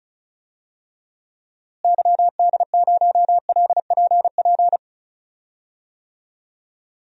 Morse code, heard as a keyed tone, about 34 wpm, 700 Hz. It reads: YD0LPP
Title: DXCC-100 Call Signs - 6 Characters - Repeated 2x - 35wpm